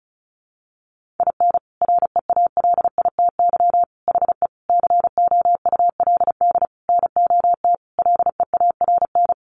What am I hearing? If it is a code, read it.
Morse code, heard as a keyed tone, about 34 wpm, 700 Hz. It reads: IN REALITY HE COULD NOT LEARN
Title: Sentences from Top 1200 Words 35wpm